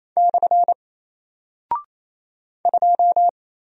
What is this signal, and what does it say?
Morse code, heard as a keyed tone, about 28 wpm, 700 Hz. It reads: / 2
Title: Single Character (Letter:Number:Punctuation:Prosign) Alt 28wpm